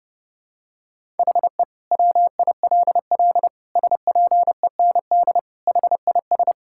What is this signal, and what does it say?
Morse code, heard as a keyed tone, about 30 wpm, 700 Hz. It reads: HE WILL SPEND HIS
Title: Sentences from Top 1000 Words - Review Part 5 - 30wpm